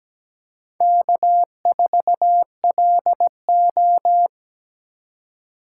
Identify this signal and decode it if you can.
Morse code, heard as a keyed tone, about 17 wpm, 700 Hz. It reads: K4LO
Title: USA Call Signs - 4 Characters - 2x Character Spacing - 17wpm